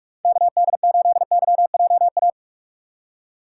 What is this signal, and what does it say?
Morse code, heard as a keyed tone, about 46 wpm, 700 Hz. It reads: KD8YJA
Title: USA Call Signs - 2x Character Spacing - 45wpm